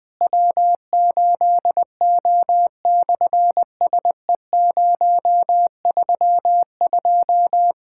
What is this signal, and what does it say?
Morse code, heard as a keyed tone, about 19 wpm, 700 Hz. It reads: W8O/SE032
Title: SOTA QSO Practice 20wpm